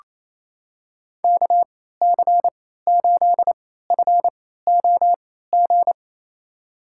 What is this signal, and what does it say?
Morse code, heard as a keyed tone, about 28 wpm, spaced as for 16 wpm, 700 Hz. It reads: KC8FOG
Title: USA Call Signs - 6 Characters - 3x Character Spacing - 28wpm